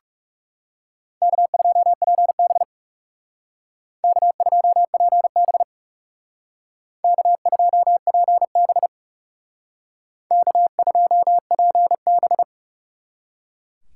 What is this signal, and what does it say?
Morse code, heard as a keyed tone, about 38 wpm, 690 Hz. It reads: K2PB K2PB K2PB K2PB